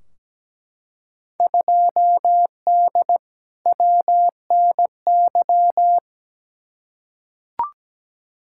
Morse code, heard as a keyed tone, about 17 wpm, 700 Hz. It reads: 2D WNY
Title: Field Day Exchanges - 17wpm